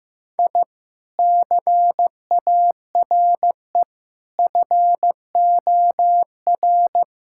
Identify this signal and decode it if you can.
Morse code, heard as a keyed tone, about 15 wpm, 700 Hz. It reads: I CARE FOR
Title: Sentences from Top 1000 Words - Review Part 6 - 15wpm